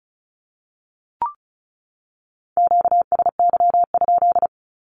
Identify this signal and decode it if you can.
Morse code, heard as a keyed tone, about 34 wpm, 700 Hz. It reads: QSY?
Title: Q-Codes 35wpm